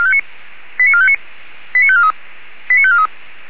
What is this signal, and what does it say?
Five-tone selective call (ZVEI), heard as quick runs of five tones, tones of about 70 ms. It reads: 89459 89543 89543